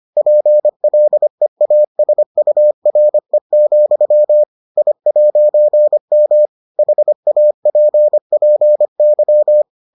Morse code, heard as a keyed tone, about 25 wpm, 600 Hz. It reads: PLEASURE, I'M HAPPY